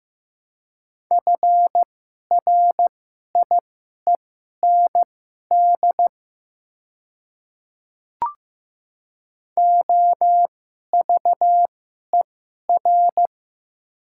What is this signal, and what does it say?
Morse code, heard as a keyed tone, about 15 wpm, spaced as for 11 wpm, 700 Hz. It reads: FRIEND OVER
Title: 2x Character Spacing - Top 300 Words - 15wpm